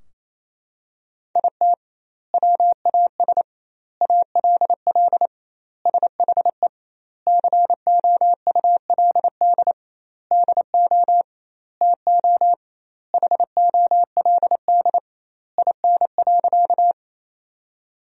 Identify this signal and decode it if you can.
Morse code, heard as a keyed tone, about 28 wpm, 700 Hz. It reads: IT WAS ALL SHE COULD DO TO HOLD IN.
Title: 2x Word-Spacing - Sentences from Top 400 Words - 28wpm